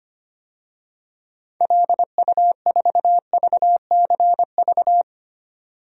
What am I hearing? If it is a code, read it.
Morse code, heard as a keyed tone, about 25 wpm, 690 Hz. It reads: LU4VCV
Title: DXCC-100 Call Signs - 6 Characters - 25wpm